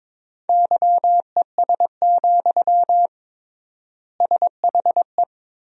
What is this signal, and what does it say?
Morse code, heard as a keyed tone, about 22 wpm, 700 Hz. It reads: YES, SHE
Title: 3x Word-Spacing - Sentences from Top 500 Words - 22wpm